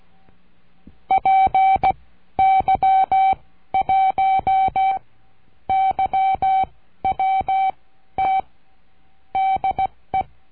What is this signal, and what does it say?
Morse code, heard as a keyed tone, about 16 wpm, spaced as for 11 wpm, 770 Hz. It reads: PY1YWT DE
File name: cursos / CURSO DE TELEGRAFIA / Telegrafia - Trilha 22